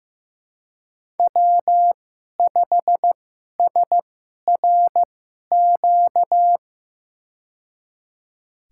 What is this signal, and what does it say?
Morse code, heard as a keyed tone, about 15 wpm, spaced as for 11 wpm, 700 Hz. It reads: W5SRQ
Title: USA Call Signs - 5 Characters - 2x Character Spacing - 15wpm